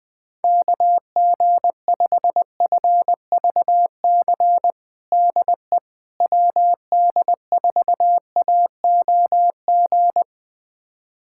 Morse code, heard as a keyed tone, about 20 wpm, 700 Hz. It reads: KG5FVC DE WD4AOG
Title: USA Callsign DE Callsign 20wpm